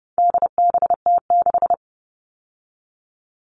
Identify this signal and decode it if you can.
Morse code, heard as a keyed tone, about 30 wpm, 700 Hz. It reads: DBT6
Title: All Cut Numbers - 4-Digit 30wpm